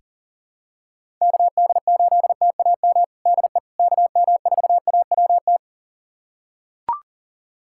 Morse code, heard as a keyed tone, about 40 wpm, 700 Hz. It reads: KD8TAM DE KM4AWT
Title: USA Callsign DE Callsign - 6 Characters - 40wpm